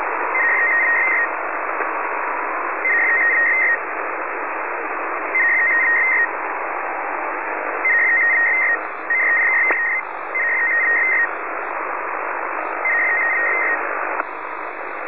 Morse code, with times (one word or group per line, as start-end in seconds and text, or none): none